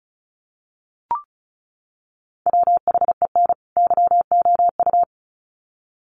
Morse code, heard as a keyed tone, about 34 wpm, 700 Hz. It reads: WHEN YOU
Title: Sets of 2 Words 35wpm